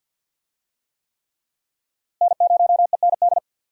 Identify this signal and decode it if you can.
Morse code, heard as a keyed tone, about 52 wpm, 700 Hz. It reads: N0END
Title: USA Call Signs - 5 Characters - Repeated 3x - 50wpm